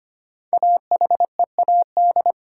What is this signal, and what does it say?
Morse code, heard as a keyed tone, about 25 wpm, 700 Hz. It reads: AHEAD
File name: 5 Letter Words - First 3 Letters Given - 25wpm